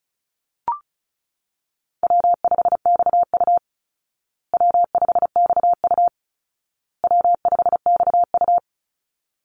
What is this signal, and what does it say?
Morse code, heard as a keyed tone, about 34 wpm, 700 Hz. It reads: W5XU W5XU W5XU
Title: Easy Call Signs Repeated 3x - 35wpm